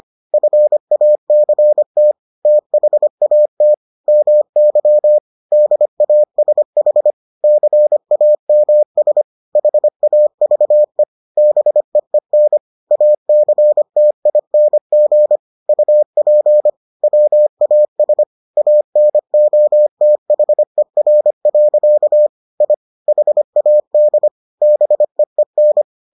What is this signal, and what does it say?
Morse code, heard as a keyed tone, about 25 wpm, 600 Hz. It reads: FACT THAT MY DASH CAMS HAVE BEEN ACTING UP WAS ANOTHER. I HAD BEEN